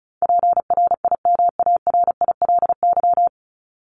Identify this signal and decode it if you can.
Morse code, heard as a keyed tone, about 34 wpm, 700 Hz. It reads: PRIMARILY
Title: Top 1001-1250 Words 35wpm